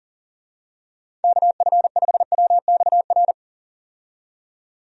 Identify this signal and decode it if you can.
Morse code, heard as a keyed tone, about 40 wpm, 700 Hz. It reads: KF5WXR
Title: USA Call Signs - 6 Characters - 40wpm